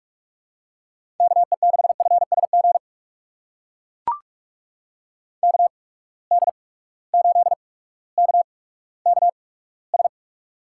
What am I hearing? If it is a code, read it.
Morse code, heard as a keyed tone, about 44 wpm, spaced as for 19 wpm, 700 Hz. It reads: KE6FSG KD8KKS
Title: USA Call Signs - 8x Character Spacing - 45wpm